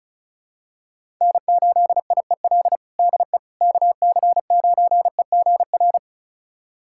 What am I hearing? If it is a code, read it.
Morse code, heard as a keyed tone, about 36 wpm, 700 Hz. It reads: N8IEL DE KC9EGR
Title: USA Callsign DE Callsign 35wpm